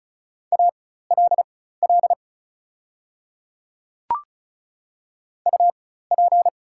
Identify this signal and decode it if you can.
Morse code, heard as a keyed tone, about 36 wpm, spaced as for 16 wpm, 700 Hz. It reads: ALL UP
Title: 4x Character Spacing - Top 100 Words - 35wpm